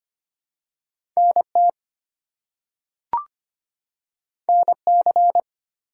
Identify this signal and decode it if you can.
Morse code, heard as a keyed tone, about 25 wpm, 700 Hz. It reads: NT NC
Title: US States and Canadian Provinces 25wpm